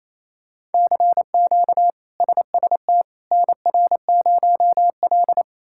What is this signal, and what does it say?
Morse code, heard as a keyed tone, about 28 wpm, 700 Hz. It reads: CQ SST NR0L
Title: K1USN SST QSO Practice - 28wpm